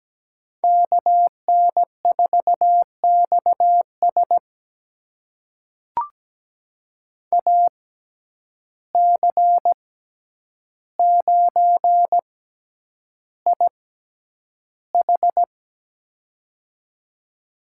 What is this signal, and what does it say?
Morse code, heard as a keyed tone, about 17 wpm, spaced as for 9 wpm, 700 Hz. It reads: KN4XS AC9IH
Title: USA Call Signs - 5 Characters - 6x Character Spacing - 17wpm